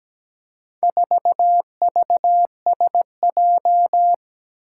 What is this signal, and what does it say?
Morse code, heard as a keyed tone, about 17 wpm, 700 Hz. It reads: 4VSJ
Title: Four Challenging Letter-Number Combinations 17wpm